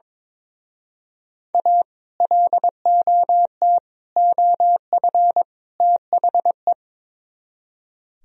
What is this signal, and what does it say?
Morse code, heard as a keyed tone, about 25 wpm, 700 Hz. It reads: A LOT OF THE
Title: Speed-Racing - Sets of 4 Words 22wpm